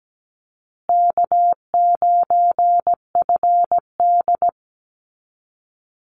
Morse code, heard as a keyed tone, about 17 wpm, 700 Hz. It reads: K9FD
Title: Easy Call Signs Repeated 2x - 17wpm